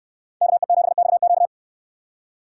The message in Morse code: KE7XQ

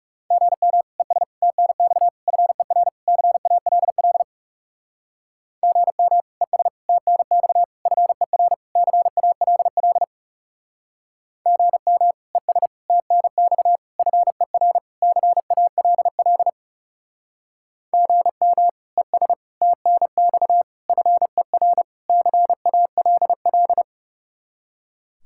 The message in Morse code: GM ES TNX FER CALL GM ES TNX FER CALL GM ES TNX FER CALL GM ES TNX FER CALL

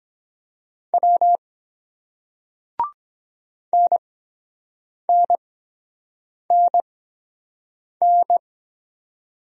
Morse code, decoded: W NNNN

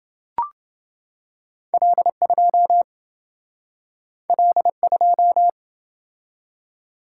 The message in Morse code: L2 L2

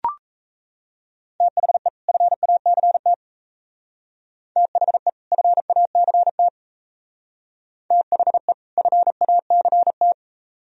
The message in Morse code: THE FACT THE FACT THE FACT

